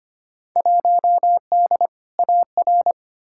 Morse code, heard as a keyed tone, about 25 wpm, 700 Hz. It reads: 1D AR
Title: Field Day Exchanges - 25wpm